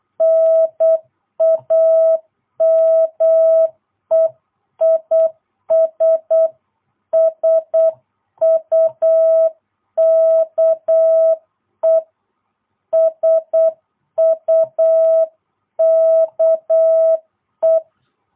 Morse code, heard as a keyed tone, about 8 wpm, 660 Hz. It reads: NAME IS SUKE SUKE